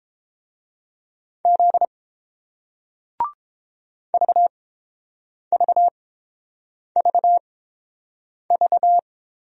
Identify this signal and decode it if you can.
Morse code, heard as a keyed tone, about 30 wpm, spaced as for 7 wpm, 700 Hz. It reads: Z VVVV